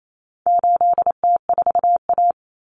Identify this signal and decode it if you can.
Morse code, heard as a keyed tone, about 28 wpm, 700 Hz. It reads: 8T4A
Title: Common Cut Numbers - 4-Digit 28wpm